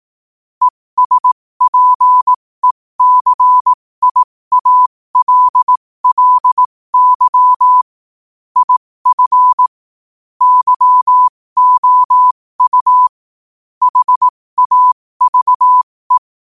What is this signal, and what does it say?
Morse code, heard as a keyed tone, about 18 wpm, spaced as for 15 wpm, 1000 Hz. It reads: ESPECIALLY IF YOU HAVE
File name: qotdfiles / 15wpm